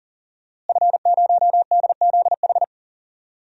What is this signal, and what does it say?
Morse code, heard as a keyed tone, about 40 wpm, 700 Hz. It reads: F0DZH
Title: DXCC-100 Call Signs - 5 Characters - Repeated 2x - 40wpm